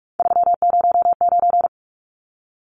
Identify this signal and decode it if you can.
Morse code, heard as a keyed tone, about 44 wpm, 700 Hz. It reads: 309
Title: Numbers - 3-Digit 45wpm